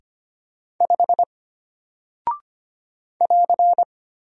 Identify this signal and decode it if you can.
Morse code, heard as a keyed tone, about 25 wpm, 700 Hz. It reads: 5 <AR>
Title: Single Character (Letter:Number:Punctuation:Prosign) Alt 25wpm